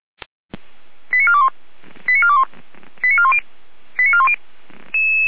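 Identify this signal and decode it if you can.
Five-tone selective call (ZVEI), heard as quick runs of five tones, tones of about 70 ms. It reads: 89421 89421 89410 89410
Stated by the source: ZVEI five-tone call 89509 received on SDR, from the feuerwehr-orlamuende.org (fire brigade Orlamünde) server.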